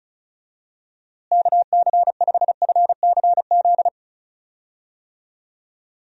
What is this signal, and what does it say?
Morse code, heard as a keyed tone, about 34 wpm, 700 Hz. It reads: KC5FCZ
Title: USA Call Signs - 6 Characters - Repeated 2x - 35wpm